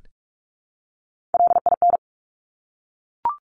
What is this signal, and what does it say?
Morse code, heard as a keyed tone, about 46 wpm, 700 Hz. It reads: LIN
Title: Top Three Letter Combinations in Words 45wpm